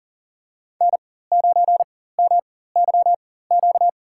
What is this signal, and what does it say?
Morse code, heard as a keyed tone, about 37 wpm, spaced as for 18 wpm, 700 Hz. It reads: N9MYQ
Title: USA Call Signs - 5 Characters - 4x Character Spacing - 40wpm